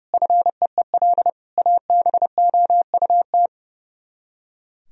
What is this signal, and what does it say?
Morse code, heard as a keyed tone, about 28 wpm, 700 Hz. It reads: FEEL ABOUT
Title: Speed-Racing - Sets of 5 Words 30wpm